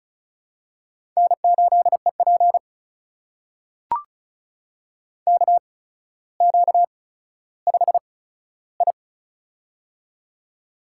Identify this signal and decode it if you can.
Morse code, heard as a keyed tone, about 36 wpm, spaced as for 15 wpm, 700 Hz. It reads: N8EP KQ5I
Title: USA Call Signs - 4 Characters - 8x Character Spacing - 35wpm